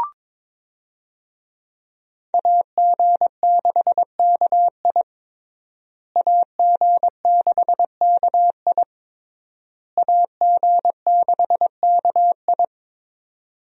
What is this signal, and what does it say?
Morse code, heard as a keyed tone, about 22 wpm, 700 Hz. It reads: AG6KI AG6KI AG6KI
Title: USA Call Signs - 5 Characters - Repeated 3x - 22wpm